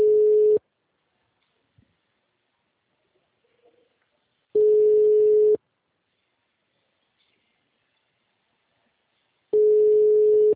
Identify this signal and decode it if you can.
Ringback tone (Europe, 425 Hz).